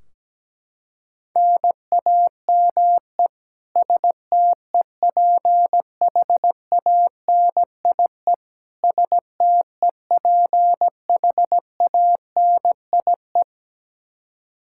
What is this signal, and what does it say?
Morse code, heard as a keyed tone, about 17 wpm, 700 Hz. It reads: NAME STEPHANIE STEPHANIE
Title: Practice QSO - Exchange 1 - Beginning - 17wpm